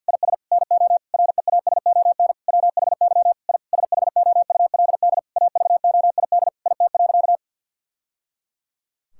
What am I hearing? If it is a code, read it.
Morse code, heard as a keyed tone, about 50 wpm, 700 Hz. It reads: IS NO REASON WHY I SHOULD AVOID IT.
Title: Sentences from Top 1000 Words - Review Part 6 - 50wpm